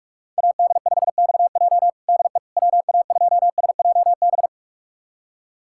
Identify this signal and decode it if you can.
Morse code, heard as a keyed tone, about 46 wpm, 700 Hz. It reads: AD5XJ DE WA2SJB
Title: USA Callsign DE Callsign 45wpm